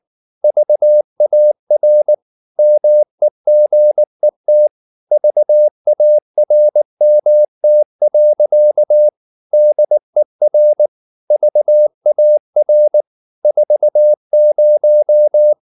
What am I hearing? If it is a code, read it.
Morse code, heard as a keyed tone, about 19 wpm, 600 Hz. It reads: VAR MEGET VARMT. DER VAR 40
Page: Ferie 19 wpm